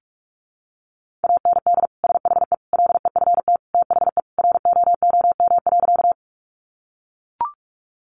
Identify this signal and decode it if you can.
Morse code, heard as a keyed tone, about 44 wpm, 700 Hz. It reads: AND SHE LEFT THE ROOM.
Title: Sentences from Top 300 Words 45wpm